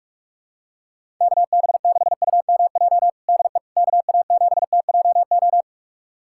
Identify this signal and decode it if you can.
Morse code, heard as a keyed tone, about 46 wpm, 700 Hz. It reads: KB6UMJ DE KA7TJO
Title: USA Callsign DE Callsign - 6 Characters - 45wpm